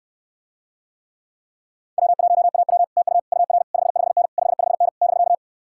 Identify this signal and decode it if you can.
Morse code, heard as a keyed tone, about 69 wpm, 700 Hz. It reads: W1AW TU UR 55N 55N <BK>